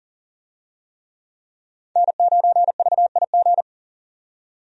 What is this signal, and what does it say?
Morse code, heard as a keyed tone, about 40 wpm, 700 Hz. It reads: N9VIG
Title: USA Call Signs - 5 Characters - 40wpm